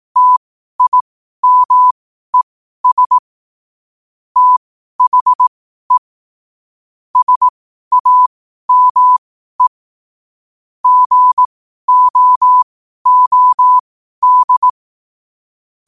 Morse code, heard as a keyed tone, about 18 wpm, spaced as for 12 wpm, 1000 Hz. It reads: TIMES THE SAME GOOD
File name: qotdfiles / 12wpm